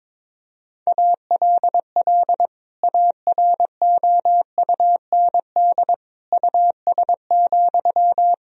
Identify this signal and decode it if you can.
Morse code, heard as a keyed tone, about 22 wpm, 700 Hz. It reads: ALL AROUND US,
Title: Sentences from Top 1000 Words - Review Part 3 - 22wpm